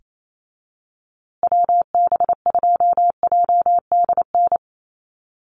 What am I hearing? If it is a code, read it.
Morse code, heard as a keyed tone, about 28 wpm, 700 Hz. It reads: WB2JDN
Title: USA Call Signs Repeated 2x 28wpm